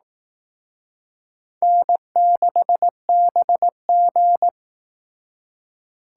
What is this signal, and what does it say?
Morse code, heard as a keyed tone, about 18 wpm, 700 Hz. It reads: N6BG
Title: Speed Racing - USA 1x2 Call Signs 15wpm